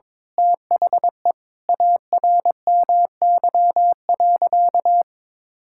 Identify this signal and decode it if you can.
Morse code, heard as a keyed tone, about 22 wpm, 700 Hz. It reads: THE ARMY.
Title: Sentences from Top 1000 Words - Review Part 6 - 22wpm